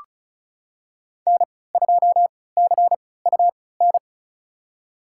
Morse code, heard as a keyed tone, about 36 wpm, spaced as for 20 wpm, 700 Hz. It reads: N2CUN
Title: USA Call Signs - 5 Characters - 3x Character Spacing - 35wpm